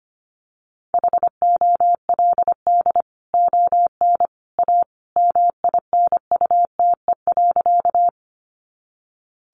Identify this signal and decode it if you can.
Morse code, heard as a keyed tone, about 25 wpm, 700 Hz. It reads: HOLD ON A MINUTE.
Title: Sentences from Top 500 Words 25wpm